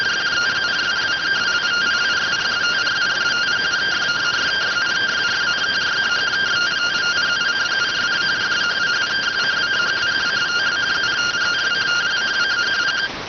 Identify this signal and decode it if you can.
Two-tone teleprinter signal, about 170 Hz shift.